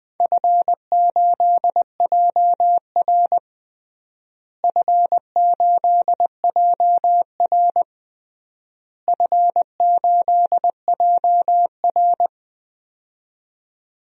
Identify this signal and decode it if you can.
Morse code, heard as a keyed tone, about 20 wpm, 700 Hz. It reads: F8JR F8JR F8JR
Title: DXCC-100 Call Signs - 4 Characters - Repeated 3x - 20wpm